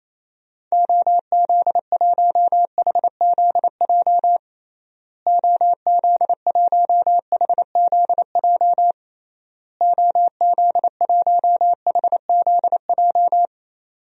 Morse code, heard as a keyed tone, about 28 wpm, 700 Hz. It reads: OZ1HZJ OZ1HZJ OZ1HZJ
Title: DXCC-100 Call Signs - 6 Characters - Repeated 3x - 28wpm